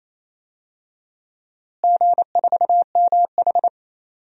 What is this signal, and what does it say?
Morse code, heard as a keyed tone, about 28 wpm, 700 Hz. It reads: G4MH
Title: DXCC-100 Call Signs - 4 Characters - Repeated 3x - 28wpm